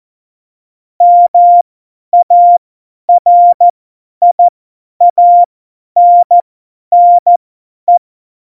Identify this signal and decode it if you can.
Morse code, heard as a keyed tone, about 14 wpm, spaced as for 10 wpm, 700 Hz. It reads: MARIANNE
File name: Morsekurs / Witze / Witz 16. April 2019, 50 F70